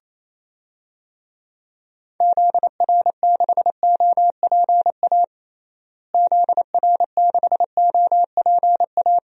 Morse code, heard as a keyed tone, about 28 wpm, 700 Hz. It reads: ZR6OPA ZR6OPA
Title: DXCC-100 Call Signs - 6 Characters - Repeated 2x - 28wpm